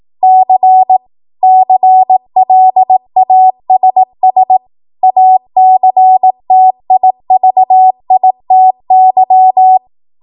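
Morse code, heard as a keyed tone, about 18 wpm, 750 Hz. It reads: C CLASS ACTIVITY
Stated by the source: 18 WPM Morse code practice from a national amateur radio association's code archive.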